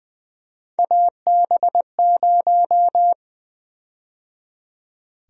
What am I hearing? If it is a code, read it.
Morse code, heard as a keyed tone, about 20 wpm, 700 Hz. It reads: AB0
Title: USA Call Sign Prefixes - 20wpm